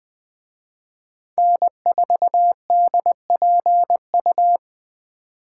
Morse code, heard as a keyed tone, about 20 wpm, 700 Hz. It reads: N4DPU